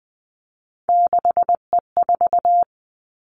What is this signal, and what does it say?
Morse code, heard as a keyed tone, about 19 wpm, 700 Hz. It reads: 6E4
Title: Three Challenging Letter-Number Combinations 20wpm